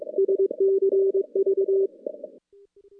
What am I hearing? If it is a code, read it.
Morse code, heard as a keyed tone, about 19 wpm, 410 Hz. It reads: SCV
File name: contests / CW-SS-k5tr- -2300